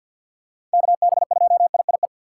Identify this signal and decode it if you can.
Morse code, heard as a keyed tone, about 47 wpm, 700 Hz. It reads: KB2IIE
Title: USA Call Signs - 6 Characters - 4x Character Spacing - 50wpm